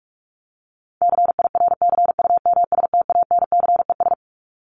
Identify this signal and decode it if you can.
Morse code, heard as a keyed tone, about 44 wpm, 700 Hz. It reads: CIRCUMSTANCES
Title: ES:S Suffix - Top 100 Words with Base Words 45wpm